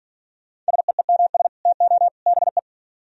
Morse code, heard as a keyed tone, about 53 wpm, 700 Hz. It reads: SEEMS TO BE